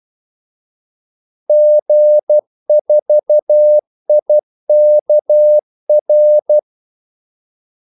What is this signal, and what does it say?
Morse code, heard as a keyed tone, about 12 wpm, 600 Hz. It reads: G4IKR
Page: 2 ekstra ordmellemrum mellem kaldesignalerne